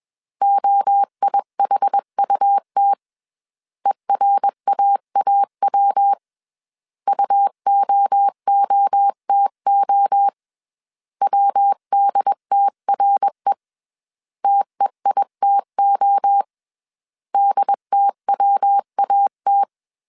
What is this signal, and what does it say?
Morse code, heard as a keyed tone, about 21 wpm, 800 Hz. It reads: OIHUT ERAAW UOOTO WDTRE TEITO DTWAT
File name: CW / DET / 1 3 1500C S B 20wpm 1754